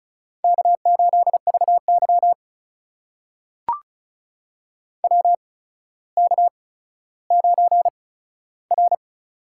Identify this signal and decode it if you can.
Morse code, heard as a keyed tone, about 35 wpm, spaced as for 15 wpm, 700 Hz. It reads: K8VY WK9R
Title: USA Call Signs - 4 Characters - 8x Character Spacing - 35wpm